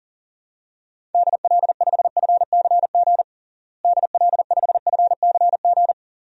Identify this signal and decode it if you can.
Morse code, heard as a keyed tone, about 40 wpm, 700 Hz. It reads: DL5FCG DL5FCG
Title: DXCC-100 Call Signs - 6 Characters - Repeated 2x - 40wpm